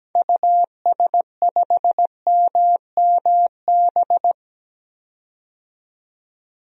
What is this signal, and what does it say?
Morse code, heard as a keyed tone, about 17 wpm, 700 Hz. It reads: US5MMB
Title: DXCC-100 Call Signs - 6 Characters - 17wpm